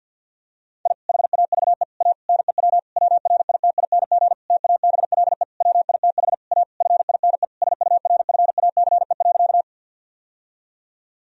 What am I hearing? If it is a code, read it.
Morse code, heard as a keyed tone, about 52 wpm, 700 Hz. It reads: I HAVE A NEW WRITING TABLE WITH A FINE SURFACE.
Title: Sentences from Top 1000 Words - Review Part 3 - 50wpm